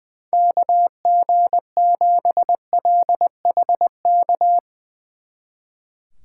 Morse code, heard as a keyed tone, about 22 wpm, 700 Hz. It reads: KG7LHK
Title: Speed Racing - USA Call Signs 20wpm